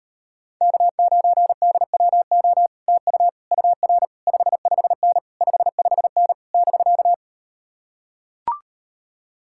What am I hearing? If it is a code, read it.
Morse code, heard as a keyed tone, about 38 wpm, 700 Hz. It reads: K9DWO TU UR 55N 55N <BK>